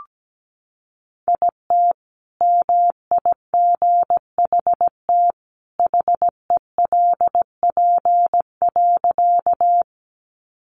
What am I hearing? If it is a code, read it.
Morse code, heard as a keyed tone, about 17 wpm, 700 Hz. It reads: IT MIGHT HELP.
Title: Sentences from Top 300 Words 17wpm